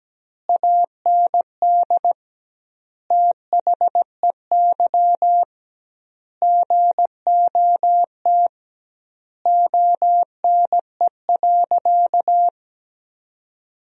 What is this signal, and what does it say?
Morse code, heard as a keyed tone, about 17 wpm, 700 Hz. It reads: AND THEY GOT ONE.
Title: 2x Word-Spacing - Sentences from Top 300 Words - 17wpm